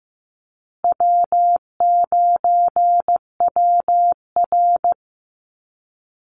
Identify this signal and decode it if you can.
Morse code, heard as a keyed tone, about 15 wpm, 700 Hz. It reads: W9WR
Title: Call Signs Repeated 3x - 15wpm